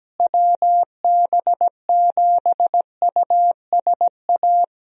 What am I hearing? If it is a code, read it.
Morse code, heard as a keyed tone, about 17 wpm, 700 Hz. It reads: WB7USA
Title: USA Call Signs - 6 Characters - Repeated 3x - 17wpm